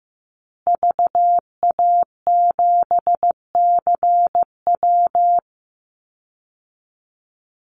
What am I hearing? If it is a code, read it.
Morse code, heard as a keyed tone, about 15 wpm, 700 Hz. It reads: VA7CW
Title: Call Signs Repeated 3x - 15wpm